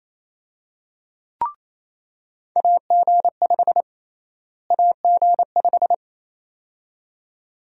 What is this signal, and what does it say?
Morse code, heard as a keyed tone, about 28 wpm, 700 Hz. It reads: AG5 AG5
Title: USA Call Sign Prefixes - Repeated 2x - 28wpm